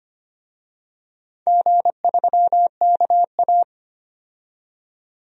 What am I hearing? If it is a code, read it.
Morse code, heard as a keyed tone, about 25 wpm, 700 Hz. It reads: G3KA
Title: DXCC-100 Call Signs - 4 Characters - 25wpm